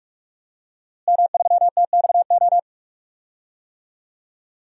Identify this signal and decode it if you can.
Morse code, heard as a keyed tone, about 46 wpm, 700 Hz. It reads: M3TXO
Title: DXCC-100 Call Signs - 5 Characters - Repeated 2x - 45wpm